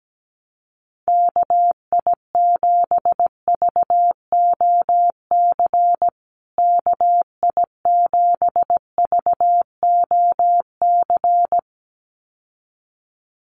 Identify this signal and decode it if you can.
Morse code, heard as a keyed tone, about 17 wpm, 700 Hz. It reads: KI7VOC KI7VOC
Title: USA Call Signs Repeated 2x 17wpm